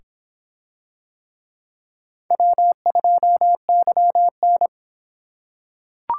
Morse code, heard as a keyed tone, about 26 wpm, 700 Hz. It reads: W2YN